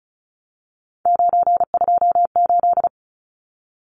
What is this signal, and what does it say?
Morse code, heard as a keyed tone, about 34 wpm, 700 Hz. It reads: 928
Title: Numbers - 3-Digit 35wpm